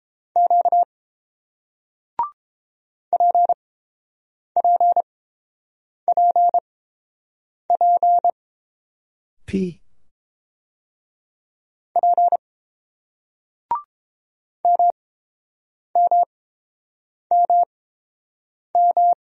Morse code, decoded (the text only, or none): Q PPPP P MMMM